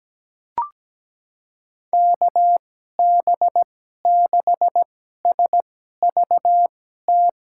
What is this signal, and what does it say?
Morse code, heard as a keyed tone, about 17 wpm, spaced as for 12 wpm, 700 Hz. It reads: KB6SVT
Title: USA Call Signs - 6 Characters - 2x Character Spacing - 17wpm